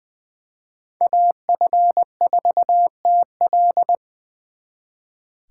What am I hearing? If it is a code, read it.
Morse code, heard as a keyed tone, about 24 wpm, 700 Hz. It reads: AF4TL
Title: Speed Racing - USA 2x2 Call Signs 20wpm